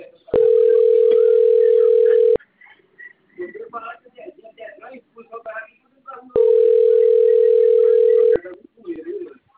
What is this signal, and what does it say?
Ringback tone (North America), 440 + 480 Hz, with speech.
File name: out- - - - - .47047